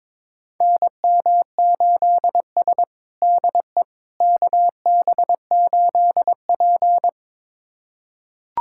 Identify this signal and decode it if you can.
Morse code, heard as a keyed tone, about 21 wpm, 700 Hz. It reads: NM8S DE KB8P
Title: USA Callsign DE Callsign - 4 Characters - 22wpm